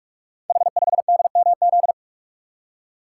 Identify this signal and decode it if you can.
Morse code, heard as a keyed tone, about 44 wpm, 700 Hz. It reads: H5DMZ
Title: Code Groups - 5 Letter:Numbers - 45wpm